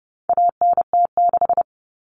Morse code, heard as a keyed tone, about 30 wpm, 700 Hz. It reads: ANT6
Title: Common Cut Numbers - 4-Digit 30wpm